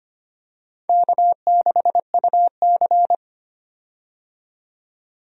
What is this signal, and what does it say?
Morse code, heard as a keyed tone, about 25 wpm, 700 Hz. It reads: K6UC
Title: USA Call Signs - 4 Characters - 25wpm